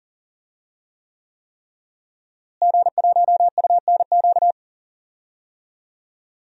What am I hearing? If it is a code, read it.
Morse code, heard as a keyed tone, about 39 wpm, 700 Hz. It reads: G1UNQ
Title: DXCC-100 Call Signs - 5 Characters - Repeated 3x - 40wpm